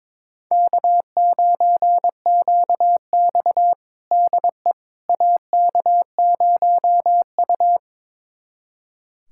Morse code, heard as a keyed tone, about 22 wpm, 700 Hz. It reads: K9QX DE AK0U
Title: USA Callsign DE Callsign - 4 Characters - 22wpm